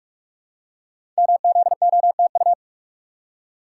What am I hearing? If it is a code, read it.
Morse code, heard as a keyed tone, about 44 wpm, 700 Hz. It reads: MZOTU
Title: Code Groups - 5 Letters - 45wpm